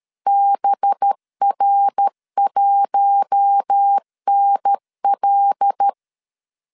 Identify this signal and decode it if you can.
Morse code, heard as a keyed tone, about 13 wpm, 800 Hz. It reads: BR1NL
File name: CW / DET / 1 9 1500C S A 12wpm 3800